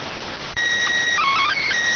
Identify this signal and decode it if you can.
SSTV (slow-scan television), PD 290.